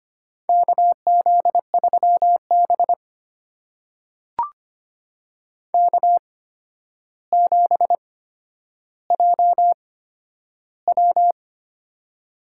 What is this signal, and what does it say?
Morse code, heard as a keyed tone, about 25 wpm, spaced as for 11 wpm, 700 Hz. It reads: KZ3B K7JW
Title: USA Call Signs - 4 Characters - 8x Character Spacing - 25wpm